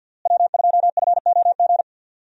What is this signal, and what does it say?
Morse code, heard as a keyed tone, about 46 wpm, 700 Hz. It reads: W2FOG
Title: USA Call Signs - 5 Characters - Repeated 3x - 50wpm